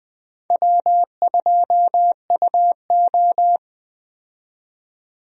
Morse code, heard as a keyed tone, about 20 wpm, 700 Hz. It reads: W2UO
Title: USA Call Signs - 4 Characters - 20wpm